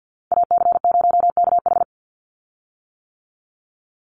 Morse code, heard as a keyed tone, about 49 wpm, 700 Hz. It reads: AC0KH
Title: USA Call Signs 50wpm